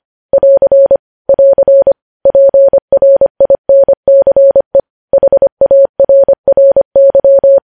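Morse code, heard as a keyed tone, about 25 wpm, 550 Hz. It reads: <AR> <AR> PRINCE HARRY
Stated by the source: Morse code news headlines sent at 25 wpm.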